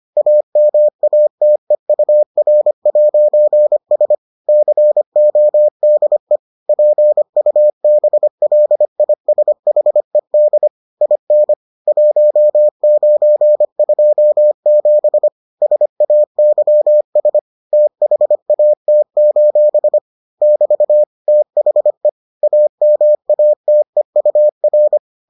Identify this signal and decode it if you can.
Morse code, heard as a keyed tone, about 25 wpm, 600 Hz. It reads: AMATEUR'S CODE PUBLISHED IN 1927 SAYS THAT: <BT> THE AMATEUR